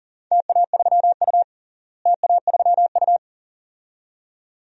Morse code, heard as a keyed tone, about 40 wpm, 700 Hz. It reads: TA3U TA3U
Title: DXCC-100 Call Signs - 4 Characters - Repeated 2x - 40wpm